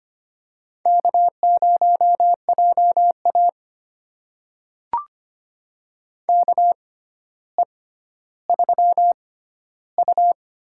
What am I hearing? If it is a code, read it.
Morse code, heard as a keyed tone, about 25 wpm, spaced as for 13 wpm, 700 Hz. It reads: K0JA KE3U
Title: USA Call Signs - 4 Characters - 6x Character Spacing - 25wpm